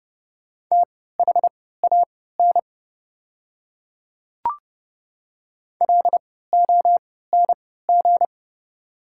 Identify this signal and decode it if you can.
Morse code, heard as a keyed tone, about 30 wpm, spaced as for 17 wpm, 700 Hz. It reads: THAN LONG
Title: 3x Character Spacing - Top 300 Words - 30wpm